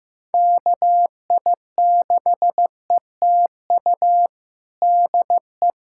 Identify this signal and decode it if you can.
Morse code, heard as a keyed tone, about 14 wpm, 700 Hz. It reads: KI6ETU DE
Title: USA Callsign DE Callsign - 6 Characters - 15wpm